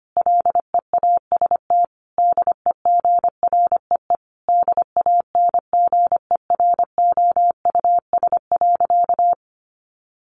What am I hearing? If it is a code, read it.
Morse code, heard as a keyed tone, about 25 wpm, 700 Hz. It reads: LEAST DEGREE DANGEROUS.
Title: Sentences from Top 1500 Words 25wpm